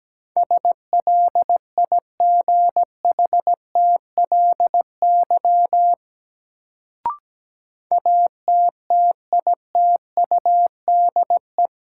Morse code, heard as a keyed tone, about 17 wpm, 700 Hz. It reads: SLIGHTLY ATTITUDE
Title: Sentences from Top 1300 Words - New Word Drill - 17wpm